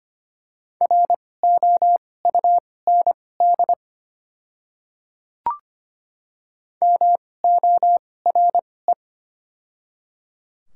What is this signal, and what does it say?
Morse code, heard as a keyed tone, about 25 wpm, spaced as for 18 wpm, 700 Hz. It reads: ROUND MORE
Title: 2x Character Spacing - Top 200 Words - 25wpm